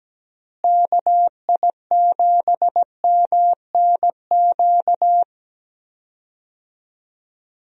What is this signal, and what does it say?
Morse code, heard as a keyed tone, about 17 wpm, 700 Hz. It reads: KI7MNQ
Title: USA Call Signs - 6 Characters - Repeated 2x - 17wpm